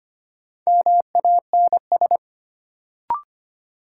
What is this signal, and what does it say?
Morse code, heard as a keyed tone, about 25 wpm, 700 Hz. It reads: MANS
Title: Sentences from Top 1000 Words - Review Part 6 - New Word Drill - 25wpm